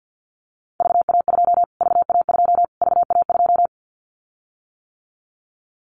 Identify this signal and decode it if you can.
Morse code, heard as a keyed tone, about 49 wpm, 700 Hz. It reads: VA2 VA2 VA2
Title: Callsign Prefixes for Top 25 DXCC - Repeated 3x - 50wpm